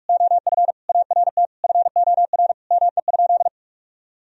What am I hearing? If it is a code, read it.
Morse code, heard as a keyed tone, about 44 wpm, 700 Hz. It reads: OF ART FOR ME?
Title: Sentences from Top 1000 Words - Review Part 3 - 45wpm